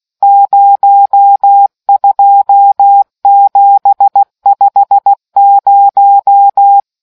Morse code, heard as a keyed tone, about 16 wpm, 800 Hz. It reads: 02750